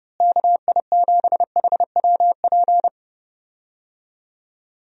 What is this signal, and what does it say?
Morse code, heard as a keyed tone, about 31 wpm, 700 Hz. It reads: KI7HWP
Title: USA Call Signs - 6 Characters - Repeated 3x - 30wpm